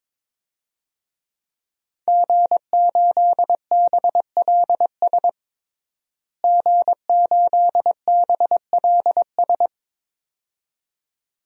Morse code, heard as a keyed tone, about 21 wpm, 700 Hz. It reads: G8BLS G8BLS
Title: DXCC-100 Call Signs - 5 Characters - Repeated 2x - 22wpm